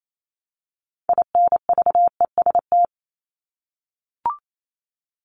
Morse code, heard as a keyed tone, about 28 wpm, 700 Hz. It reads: INVEST
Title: Top 1001-1250 Words 28wpm